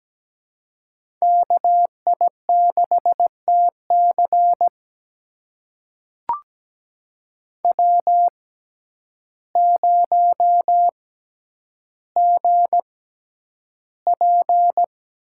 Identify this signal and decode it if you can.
Morse code, heard as a keyed tone, about 17 wpm, spaced as for 9 wpm, 700 Hz. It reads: KI6TC W0GP
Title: USA Call Signs - 5 Characters - 6x Character Spacing - 17wpm